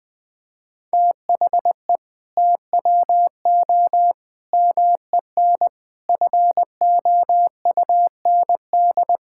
Morse code, heard as a keyed tone, about 20 wpm, 700 Hz. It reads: THE TWO MEN FOUND
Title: Sentences from Top 1000 Words - Review Part 5 - 20wpm